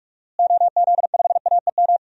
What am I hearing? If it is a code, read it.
Morse code, heard as a keyed tone, about 42 wpm, 700 Hz. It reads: OZ5AEM
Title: DXCC-100 Call Signs - 6 Characters - Repeated 3x - 45wpm